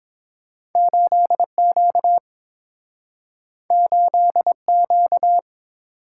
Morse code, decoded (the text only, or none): 8Q 8Q